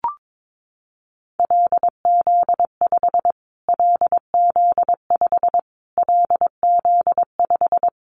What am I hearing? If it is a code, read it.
Morse code, heard as a keyed tone, about 22 wpm, 700 Hz. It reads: LZ5 LZ5 LZ5
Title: Callsign Prefixes for Top 25 DXCC - Repeated 3x - 22wpm